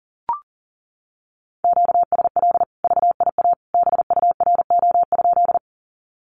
Morse code, heard as a keyed tone, about 40 wpm, 700 Hz. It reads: QSL VIA BURO?
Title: Common QSO Phrases 40wpm